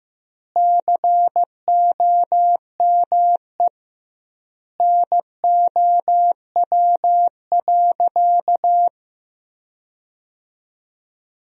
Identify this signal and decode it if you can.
Morse code, heard as a keyed tone, about 15 wpm, 700 Hz. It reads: COME NOW.
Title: 2x Word-Spacing - Sentences from Top 500 Words - 15wpm